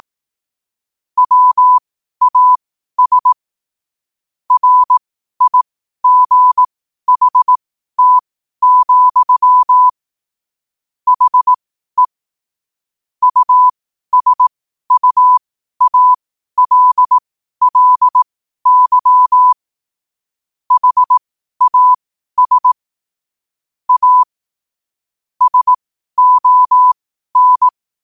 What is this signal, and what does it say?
Morse code, heard as a keyed tone, about 18 wpm, spaced as for 12 wpm, 1000 Hz. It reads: WAS RIGHT, HE USUALLY HAS A SON
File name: qotdfiles / 12wpm